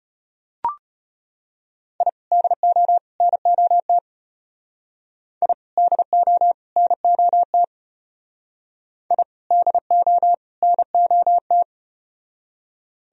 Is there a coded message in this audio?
Yes: Morse code from 2.0 to 11.6 s, heard as a keyed tone, about 33 wpm, 700 Hz. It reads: I DO NOT I DO NOT I DO NOT